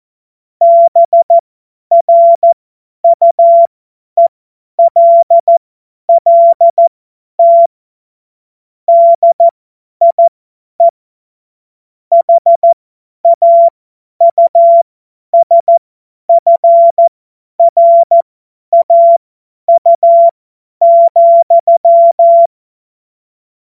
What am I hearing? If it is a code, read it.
Morse code, heard as a keyed tone, about 14 wpm, spaced as for 10 wpm, 680 Hz. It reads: BRUELLT DIE HAUSFRAU,